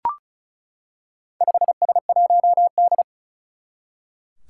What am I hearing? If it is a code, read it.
Morse code, heard as a keyed tone, about 36 wpm, 700 Hz. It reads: 5S1D
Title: Four Challenging Letter-Number Combinations 35wpm